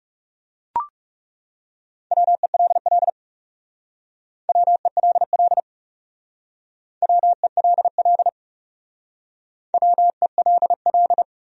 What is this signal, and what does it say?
Morse code, heard as a keyed tone, about 40 wpm, 700 Hz. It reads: WELL WELL WELL WELL